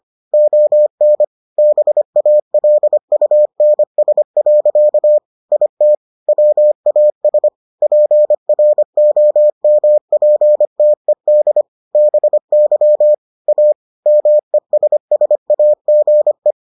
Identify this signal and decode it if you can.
Morse code, heard as a keyed tone, about 25 wpm, 600 Hz. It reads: ON BALUNS. IT WAS PROMPTED BY A MESSAGE